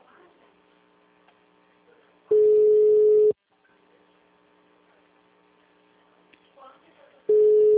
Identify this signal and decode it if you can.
Ringback tone (Europe, 425 Hz).